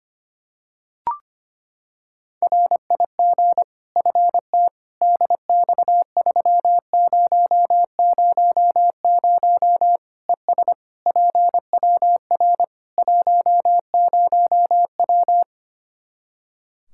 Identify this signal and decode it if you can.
Morse code, heard as a keyed tone, about 25 wpm, 700 Hz. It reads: RIG FT DX3000 ES PWR 10W
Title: Practice QSO - Exchange 2 - Info - 25wpm